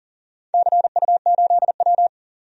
Morse code, heard as a keyed tone, about 40 wpm, 700 Hz. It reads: CU8W
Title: DXCC-100 Call Signs - 4 Characters - 40wpm